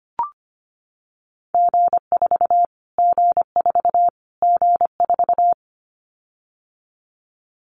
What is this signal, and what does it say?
Morse code, heard as a keyed tone, about 25 wpm, 700 Hz. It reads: G4 G4 G4
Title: Callsign Prefixes for Top 25 DXCC - Repeated 3x - 25wpm